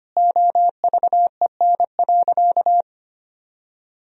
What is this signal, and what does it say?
Morse code, heard as a keyed tone, about 25 wpm, 700 Hz. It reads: OVEN.
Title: Sentences from Top 1000 Words - Review Part 4 - 25wpm